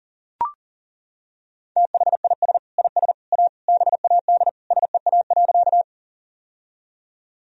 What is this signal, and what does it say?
Morse code, heard as a keyed tone, about 40 wpm, 700 Hz. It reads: THIS IS A BAD SEA.
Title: Sentences from Top 1000 Words - Review Part 1 - 40wpm